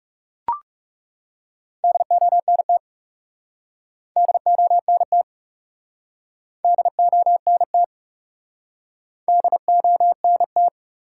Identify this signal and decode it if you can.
Morse code, heard as a keyed tone, about 40 wpm, 700 Hz. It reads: DONT DONT DONT DONT